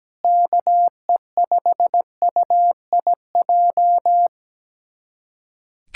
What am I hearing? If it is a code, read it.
Morse code, heard as a keyed tone, about 17 wpm, 700 Hz. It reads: KE5UIJ
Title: USA Callsign DE Callsign 17wpm